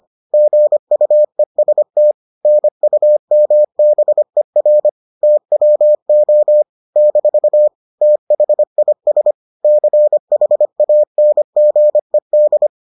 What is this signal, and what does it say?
Morse code, heard as a keyed tone, about 25 wpm, 600 Hz. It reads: GUEST NUMBER TWO - THIS CHANGED